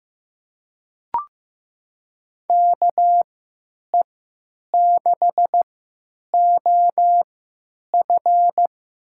Morse code, heard as a keyed tone, about 15 wpm, spaced as for 9 wpm, 700 Hz. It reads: KE6OF
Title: USA Call Signs - 3x Character Spacing - 15wpm